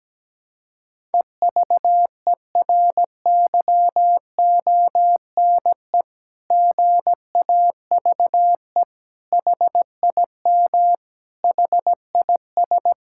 Morse code, heard as a keyed tone, about 17 wpm, 700 Hz. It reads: EVERYONE GAVE HIM HIS
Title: Sentences from Top 1000 Words - Review Part 6 - 17wpm